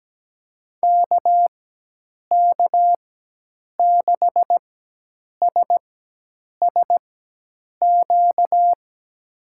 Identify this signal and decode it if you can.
Morse code, heard as a keyed tone, about 17 wpm, spaced as for 8 wpm, 700 Hz. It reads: KK6SSQ
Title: USA Call Signs - 4x Character Spacing - 17wpm